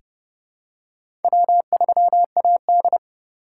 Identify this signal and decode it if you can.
Morse code, heard as a keyed tone, about 30 wpm, 700 Hz. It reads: W3AD
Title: USA Call Signs - 4 Characters - 3x Character Spacing - 30wpm